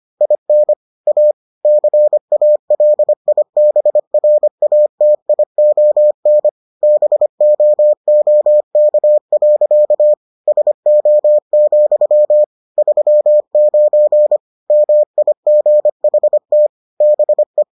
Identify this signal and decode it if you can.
Morse code, heard as a keyed tone, about 25 wpm, 600 Hz. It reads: IN A CALIBRATION BOOK. SO, 39 MIGHT BE